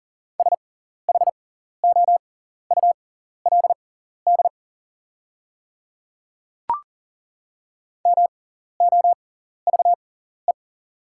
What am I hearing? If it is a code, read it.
Morse code, heard as a keyed tone, about 40 wpm, spaced as for 14 wpm, 700 Hz. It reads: SHOULD MOVE
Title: 6x Character Spacing - Top 300 Words - 40wpm